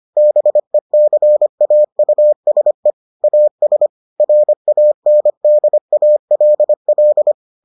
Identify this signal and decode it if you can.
Morse code, heard as a keyed tone, about 25 wpm, 600 Hz. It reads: BECAUSE AS RANDALL